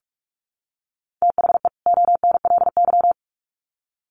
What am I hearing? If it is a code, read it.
Morse code, heard as a keyed tone, about 46 wpm, 700 Hz. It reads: THE ONLY
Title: Sets of 2 Words 45wpm